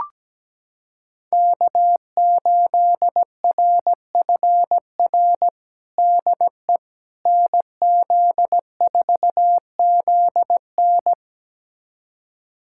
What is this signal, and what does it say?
Morse code, heard as a keyed tone, about 17 wpm, 700 Hz. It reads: K8RFR DE NZ4ZN
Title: USA Callsign DE Callsign - 5 Characters - 17wpm